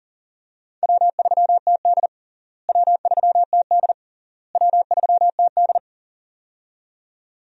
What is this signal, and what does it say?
Morse code, heard as a keyed tone, about 40 wpm, 700 Hz. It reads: W3TD W3TD W3TD
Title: USA Call Signs - 4 Characters - Repeated 3x - 40wpm